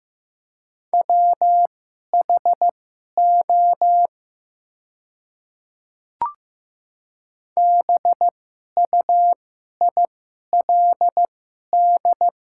Morse code, heard as a keyed tone, about 15 wpm, spaced as for 11 wpm, 700 Hz. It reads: WHO BUILD
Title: 2x Character Spacing - Top 200 Words - 15wpm